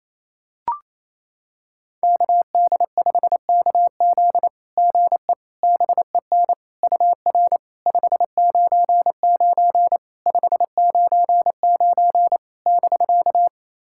Morse code, heard as a keyed tone, about 28 wpm, 690 Hz. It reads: KD5KZ GE BEN UR 599 599 <BK>
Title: SOTA QSO Practice 28wpm